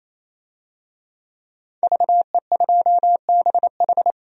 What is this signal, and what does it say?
Morse code, heard as a keyed tone, about 28 wpm, 700 Hz. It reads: VE2BH
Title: DXCC-100 Call Signs - 5 Characters - 28wpm